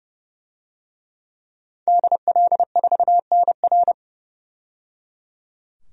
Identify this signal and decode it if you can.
Morse code, heard as a keyed tone, about 30 wpm, 700 Hz. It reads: DL4NR
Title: DXCC-100 Call Signs - 5 Characters - 30wpm